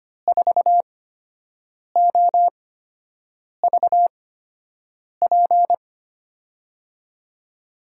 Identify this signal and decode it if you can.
Morse code, heard as a keyed tone, about 25 wpm, spaced as for 7 wpm, 700 Hz. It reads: 4OVP
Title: USA Call Signs - 6 Characters - 8x Character Spacing - 25wpm